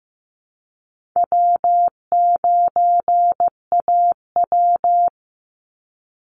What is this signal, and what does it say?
Morse code, heard as a keyed tone, about 15 wpm, 700 Hz. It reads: W9AW
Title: Easy Call Signs - 15wpm